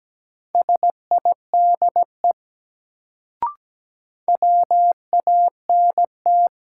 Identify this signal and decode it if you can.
Morse code, heard as a keyed tone, about 17 wpm, 700 Hz. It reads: SIDE WANT
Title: Sets of 2 Words - New Word Drill - 17wpm